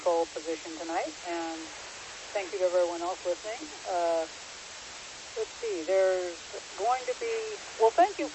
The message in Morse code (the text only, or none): none